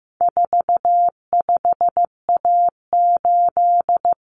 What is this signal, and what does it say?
Morse code, heard as a keyed tone, about 15 wpm, 700 Hz. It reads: 45A8
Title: Common Cut Numbers - 4-Digit 15wpm